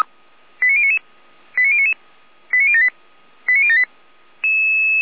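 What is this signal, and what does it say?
Five-tone selective call (ZVEI), heard as quick runs of five tones, tones of about 70 ms. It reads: 89099 89099 89078 89078
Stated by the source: ZVEI five-tone selective call received by SDR.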